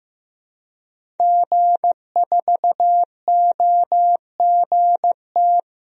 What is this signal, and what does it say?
Morse code, heard as a keyed tone, about 15 wpm, 700 Hz. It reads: G4OGT
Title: DXCC-100 Call Signs - 5 Characters - Repeated 2x - 15wpm